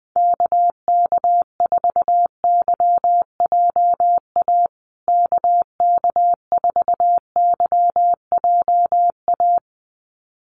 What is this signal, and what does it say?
Morse code, heard as a keyed tone, about 20 wpm, 700 Hz. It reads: KK4YJA KK4YJA
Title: USA Call Signs Repeated 2x 20wpm